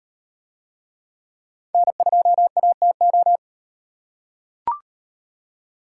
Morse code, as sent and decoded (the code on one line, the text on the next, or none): -. ..--- .- - ---
N2ATO